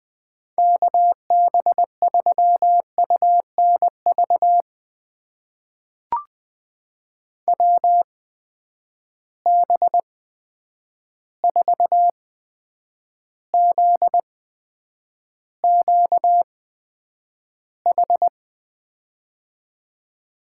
Morse code, heard as a keyed tone, about 20 wpm, spaced as for 9 wpm, 700 Hz. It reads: KB3UNV WB4ZQH
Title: USA Call Signs - 6 Characters - 8x Character Spacing - 20wpm